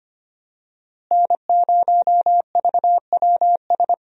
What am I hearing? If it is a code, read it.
Morse code, heard as a keyed tone, about 25 wpm, 700 Hz. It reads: N0VWS
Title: USA Call Signs - 5 Characters - Repeated 3x - 25wpm